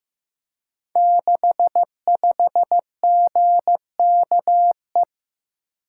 Morse code, heard as a keyed tone, about 15 wpm, 700 Hz. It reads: 65GKE
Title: Code Groups - 5 Letter:Numbers - 15wpm